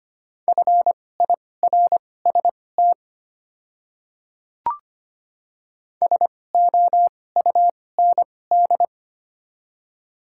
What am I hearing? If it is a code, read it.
Morse code, heard as a keyed tone, about 25 wpm, spaced as for 18 wpm, 700 Hz. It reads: FIRST SOUND
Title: 2x Character Spacing - Top 100 Words - 25wpm